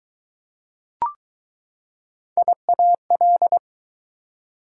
Morse code, heard as a keyed tone, about 20 wpm, 700 Hz. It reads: IAL